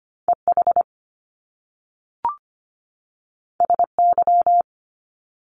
Morse code, heard as a keyed tone, about 25 wpm, 700 Hz. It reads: EH SY
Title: Top Two Letter Combinations in Words 25wpm